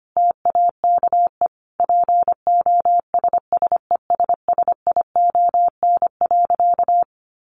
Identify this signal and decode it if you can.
Morse code, heard as a keyed tone, about 25 wpm, 700 Hz. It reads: TAKE POSSESSION.
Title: Sentences from Top 1400 Words 25wpm